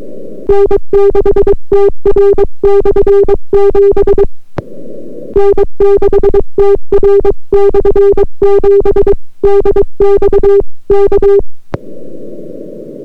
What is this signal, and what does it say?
Morse code, heard as a keyed tone, about 22 wpm, 400 Hz. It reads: N6TR/7 N6TR/7 DX K